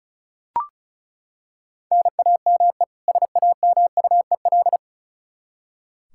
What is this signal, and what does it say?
Morse code, heard as a keyed tone, about 34 wpm, 700 Hz. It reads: NAME SAMUEL
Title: Practice QSO - Exchange 1 - Beginning - 35wpm